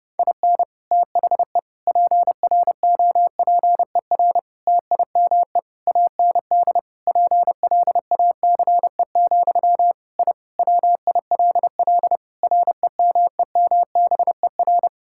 Morse code, heard as a keyed tone, about 30 wpm, 700 Hz. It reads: IN THE PROPER TIME AND PLACE, I WILL REMEMBER
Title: Sentences from Top 1000 Words - Review Part 5 - 30wpm